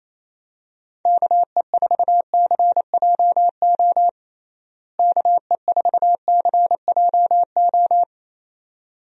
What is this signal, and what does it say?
Morse code, heard as a keyed tone, about 28 wpm, 700 Hz. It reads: KE4CJO KE4CJO
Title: USA Call Signs - 6 Characters - Repeated 2x - 28wpm